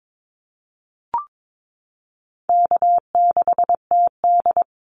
Morse code, heard as a keyed tone, about 22 wpm, 700 Hz. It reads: K6TD
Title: Easy Call Signs Repeated 2x - 22wpm